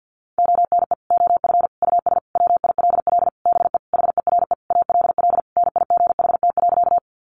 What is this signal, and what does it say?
Morse code, heard as a keyed tone, about 50 wpm, 700 Hz. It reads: ONE OF US WILL BE HERE ALL NIGHT.
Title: Sentences from Top 300 Words 50wpm